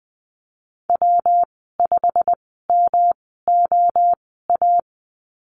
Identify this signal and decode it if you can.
Morse code, heard as a keyed tone, about 20 wpm, spaced as for 14 wpm, 700 Hz. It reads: W5MOA
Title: 219 Lesson 35g - Number 0 - Call Signs 20wpm